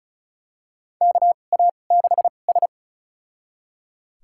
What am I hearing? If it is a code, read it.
Morse code, heard as a keyed tone, about 36 wpm, spaced as for 25 wpm, 700 Hz. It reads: KA6S
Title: USA Call Signs - 4 Characters - 2x Character Spacing - 35wpm